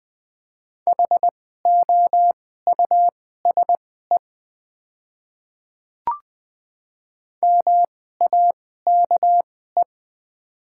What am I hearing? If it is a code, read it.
Morse code, heard as a keyed tone, about 20 wpm, spaced as for 14 wpm, 700 Hz. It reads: HOUSE MAKE
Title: 2x Character Spacing - Top 300 Words - 20wpm